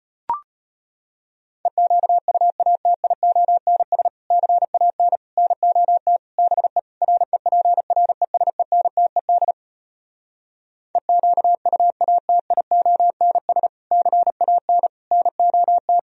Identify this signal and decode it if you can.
Morse code, heard as a keyed tone, about 33 wpm, 700 Hz. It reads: EQUATIONS CAN NOT BE REPRESENTED EQUATIONS CAN NOT